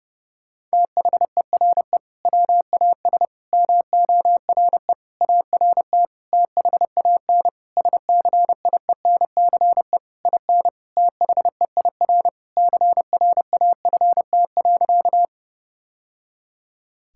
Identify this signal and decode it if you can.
Morse code, heard as a keyed tone, about 30 wpm, 700 Hz. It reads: THERE WAS MORE ART THAN SCIENCE IN THEIR CRAFT.
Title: Sentences from Top 1000 Words - Review Part 6 - 30wpm